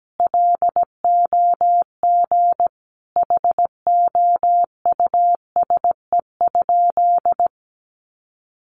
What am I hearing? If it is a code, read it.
Morse code, heard as a keyed tone, about 17 wpm, 700 Hz. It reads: LOG HOUSE?
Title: Sentences from Top 1400 Words 17wpm